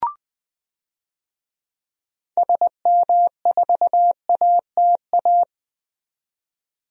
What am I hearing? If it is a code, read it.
Morse code, heard as a keyed tone, about 20 wpm, 700 Hz. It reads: SM4ATA
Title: DXCC-100 Call Signs - 6 Characters - 20wpm